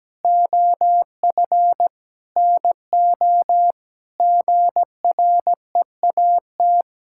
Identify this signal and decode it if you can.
Morse code, heard as a keyed tone, about 17 wpm, 700 Hz. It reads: OF NO GREAT
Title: Sentences from Top 1600 Words 17wpm